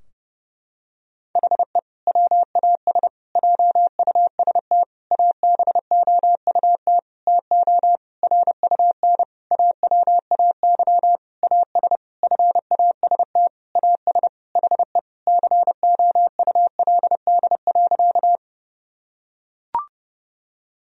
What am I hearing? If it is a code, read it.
Morse code, heard as a keyed tone, about 32 wpm, 700 Hz. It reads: HE WAS JUST ABOUT TO RUN AWAY AS FAST AS HE COULD.
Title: Sentences from Top 1000 Words - Review Part 2 - 30wpm